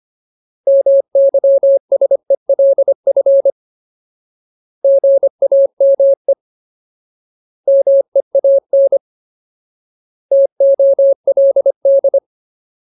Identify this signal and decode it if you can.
Morse code, heard as a keyed tone, about 25 wpm, 550 Hz. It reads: MYSELF GAME MEAN TOLD
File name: Ultimate Word List / 25WPM Ultimate